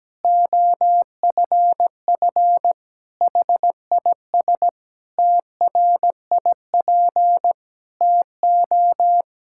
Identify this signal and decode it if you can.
Morse code, heard as a keyed tone, about 17 wpm, 700 Hz. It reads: OFF HIS TRIP TO
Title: Pangram Challenge v2 - 17wpm